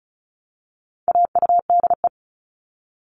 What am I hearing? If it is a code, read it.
Morse code, heard as a keyed tone, about 34 wpm, 700 Hz. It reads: AUDE
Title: All Cut Numbers - 4-Digit 35wpm